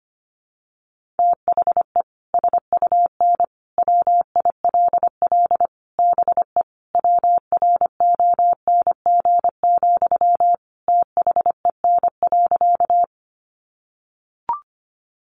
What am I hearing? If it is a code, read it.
Morse code, heard as a keyed tone, about 25 wpm, 700 Hz. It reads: THE SUN WILL BE WRONG, THEN.
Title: Sentences from Top 900 Words 25wpm